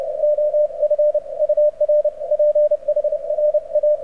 Morse code, heard as a keyed tone, about 31 wpm, 590 Hz. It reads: OF UR PHRA
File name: archived / CRSnet